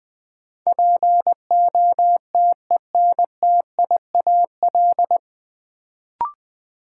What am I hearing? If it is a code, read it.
Morse code, heard as a keyed tone, about 20 wpm, 700 Hz. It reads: POTENTIAL
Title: Sentences from Top 1000 Words - Review Part 4 - New Word Drill - 20wpm